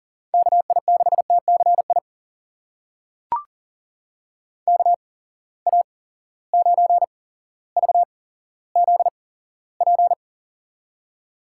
Morse code, heard as a keyed tone, about 42 wpm, spaced as for 17 wpm, 700 Hz. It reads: KI6TCI KA9VZP
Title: USA Call Signs - 6 Characters - 8x Character Spacing - 40wpm